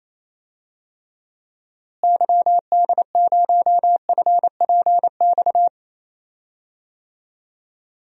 Morse code, heard as a keyed tone, about 28 wpm, 700 Hz. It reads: YD0FPX
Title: DXCC-100 Call Signs - 6 Characters - Repeated 3x - 28wpm